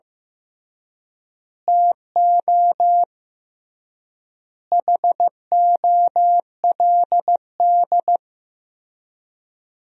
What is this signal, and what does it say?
Morse code, heard as a keyed tone, about 15 wpm, 700 Hz. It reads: TO HOLD
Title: 3x Word-Spacing - Sentences from Top 400 Words - 15wpm